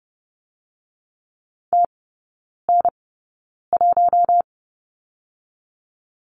Morse code, heard as a keyed tone, about 30 wpm, 700 Hz. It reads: T N 1
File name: Lesson 19e - Letter w - Three Characters 30wpm